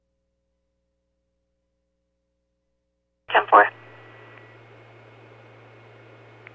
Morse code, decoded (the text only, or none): none